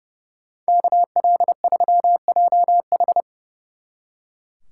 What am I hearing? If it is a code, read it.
Morse code, heard as a keyed tone, about 36 wpm, 700 Hz. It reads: KL3JH
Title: Speed Racing - USA Call Signs 30wpm